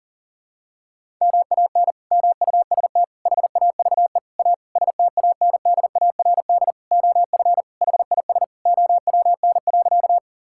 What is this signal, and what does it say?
Morse code, heard as a keyed tone, about 40 wpm, 700 Hz. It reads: MAN MUST HAVE A STANDARD OF HIS OWN.
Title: Sentences from Top 1000 Words - Review Part 6 - 40wpm